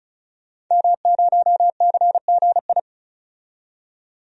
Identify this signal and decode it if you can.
Morse code, heard as a keyed tone, about 36 wpm, 700 Hz. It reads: M0CGI
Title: DXCC-100 Call Signs - 5 Characters - 35wpm